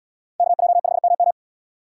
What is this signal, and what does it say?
Morse code, heard as a keyed tone, about 73 wpm, 700 Hz. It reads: KJ5MM